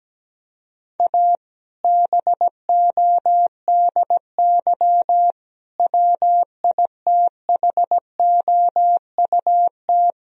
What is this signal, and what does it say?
Morse code, heard as a keyed tone, about 17 wpm, 700 Hz. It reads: A BODY WITHOUT
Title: Sentences from Top 1000 Words - Review Part 3 - 17wpm